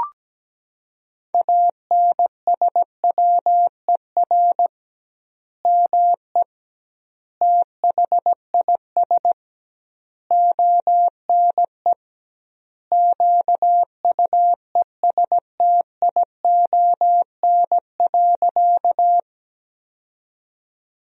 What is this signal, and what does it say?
Morse code, heard as a keyed tone, about 17 wpm, 700 Hz. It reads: ANSWER ME THIS ONE QUESTION.
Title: 2x Word-Spacing - Sentences from Top 400 Words - 17wpm